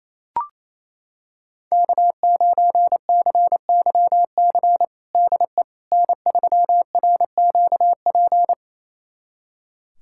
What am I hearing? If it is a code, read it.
Morse code, heard as a keyed tone, about 27 wpm, 700 Hz. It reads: K9CYC DE N3RQP
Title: USA Callsign DE Callsign - 5 Characters - 28wpm